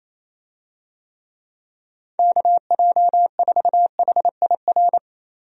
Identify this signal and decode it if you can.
Morse code, heard as a keyed tone, about 28 wpm, 700 Hz. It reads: KJ4HIR
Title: USA Call Signs - 6 Characters - Repeated 3x - 28wpm